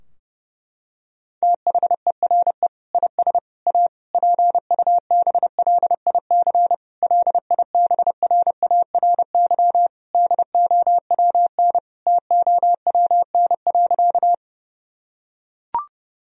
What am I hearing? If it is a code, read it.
Morse code, heard as a keyed tone, about 30 wpm, 700 Hz. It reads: THERE IS A PUBLIC LIBRARY DOWN TOWN.
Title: Sentences from Top 1000 Words - Review Part 4 - 30wpm